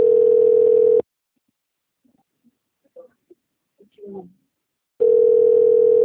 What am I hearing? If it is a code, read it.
Ringback tone (North America), 440 + 480 Hz, with speech.
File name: out- - - - - .27136